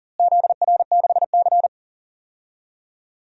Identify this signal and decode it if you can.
Morse code, heard as a keyed tone, about 40 wpm, 700 Hz. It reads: ZR6C
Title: DXCC-100 Call Signs - 4 Characters - Repeated 3x - 40wpm